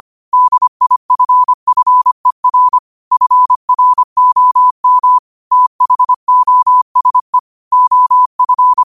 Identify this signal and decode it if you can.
Morse code, heard as a keyed tone, about 25 wpm, 1000 Hz. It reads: DIFFER FROM THOSE OF
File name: qotdfiles / 25wpm